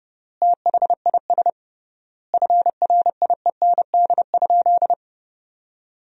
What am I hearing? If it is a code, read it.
Morse code, heard as a keyed tone, about 30 wpm, 700 Hz. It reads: THIS FRIEND?
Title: 3x Word-Spacing - Sentences from Top 300 Words - 30wpm